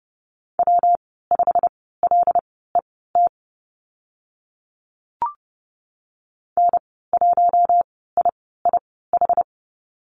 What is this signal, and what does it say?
Morse code, heard as a keyed tone, about 30 wpm, spaced as for 17 wpm, 700 Hz. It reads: W5LET N1IIH
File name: Lesson 19g - Letter w - Call Signs 30wpm